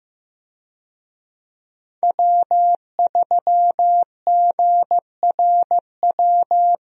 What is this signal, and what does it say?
Morse code, heard as a keyed tone, about 15 wpm, 700 Hz. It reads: W3GRW
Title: USA Call Signs - 5 Characters - Repeated 2x - 15wpm